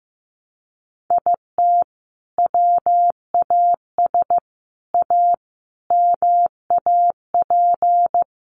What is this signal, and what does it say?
Morse code, heard as a keyed tone, about 15 wpm, 700 Hz. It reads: IT WAS A MAP
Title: Sentences from Top 1300 Words 15wpm